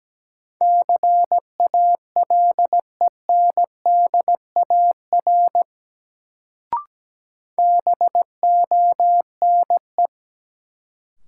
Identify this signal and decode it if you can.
Morse code, heard as a keyed tone, about 17 wpm, 700 Hz. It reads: CALENDAR BONE
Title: Sentences from Top 1600 Words - New Word Drill Alt - 17wpm